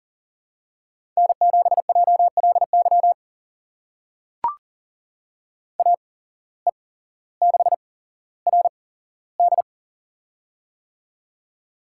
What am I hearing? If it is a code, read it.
Morse code, heard as a keyed tone, about 44 wpm, spaced as for 18 wpm, 700 Hz. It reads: N7JLY AE6RD
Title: USA Call Signs - 8x Character Spacing - 40wpm